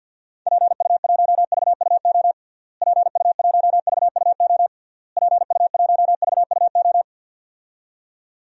Morse code, heard as a keyed tone, about 52 wpm, 700 Hz. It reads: PU1VUO PU1VUO PU1VUO
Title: DXCC-100 Call Signs - 6 Characters - Repeated 3x - 50wpm